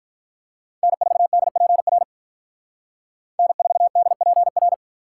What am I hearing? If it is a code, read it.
Morse code, heard as a keyed tone, about 43 wpm, 700 Hz. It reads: N4DPR N4DPR
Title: Speed Racing - USA 1x3 Call Signs 35wpm